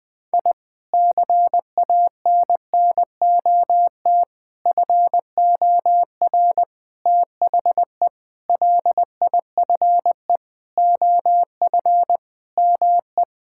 Morse code, heard as a keyed tone, about 20 wpm, 700 Hz. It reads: I CANNOT FOR THE LIFE OF ME
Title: Sentences with 8-Word Phrases - Spoken Beginning - 20wpm